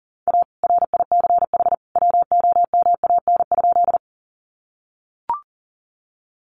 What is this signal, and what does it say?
Morse code, heard as a keyed tone, about 40 wpm, 700 Hz. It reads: A RICH WOMAN?
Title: Sentences from Top 1100 Words 40wpm